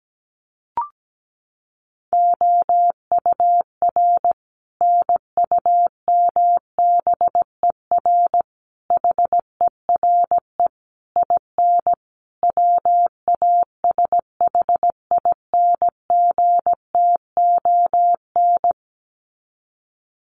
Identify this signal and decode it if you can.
Morse code, heard as a keyed tone, about 17 wpm, 700 Hz. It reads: OUR NUMBER HERE IN WASHINGTON
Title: Sets of 5 Words Encore2 17wpm